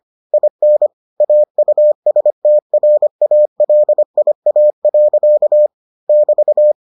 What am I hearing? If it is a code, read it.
Morse code, heard as a keyed tone, about 25 wpm, 600 Hz. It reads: IN AUSTRALIA. <BT>